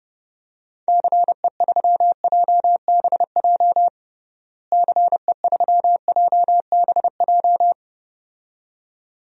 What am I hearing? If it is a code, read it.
Morse code, heard as a keyed tone, about 32 wpm, 700 Hz. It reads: CE3JBJ CE3JBJ
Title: DXCC-100 Call Signs - 6 Characters - Repeated 2x - 30wpm